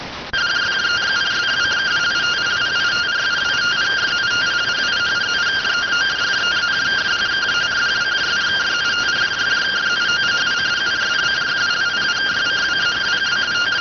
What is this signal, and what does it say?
Two-tone teleprinter signal, about 170 Hz shift.